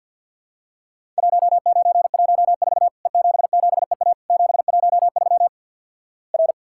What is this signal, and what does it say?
Morse code, heard as a keyed tone, about 50 wpm, 700 Hz. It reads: 1914 E77EA 713 R
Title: Worked All Europe DX Contest - QTC Exchange with Repeats - 50wpm